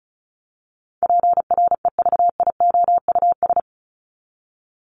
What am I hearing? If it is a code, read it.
Morse code, heard as a keyed tone, about 34 wpm, 700 Hz. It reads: PREVIOUS
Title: Top 1251-1500 Words 35wpm